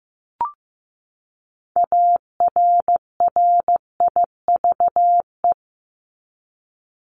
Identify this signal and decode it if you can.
Morse code, heard as a keyed tone, about 15 wpm, 700 Hz. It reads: ARRIVE
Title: ED Suffix - Top 100 Words with Base Words 15wpm